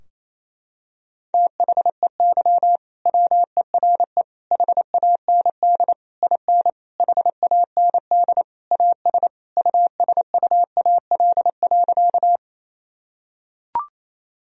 Morse code, heard as a keyed tone, about 28 wpm, 700 Hz. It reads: THEY WERE HAND IN HAND AS USUAL.
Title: Sentences from Top 1000 Words - Review Part 2 - 28wpm